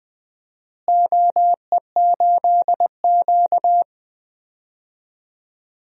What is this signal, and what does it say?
Morse code, heard as a keyed tone, about 20 wpm, 700 Hz. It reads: OE8Q
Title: DXCC-100 Call Signs - 4 Characters - Repeated 3x - 20wpm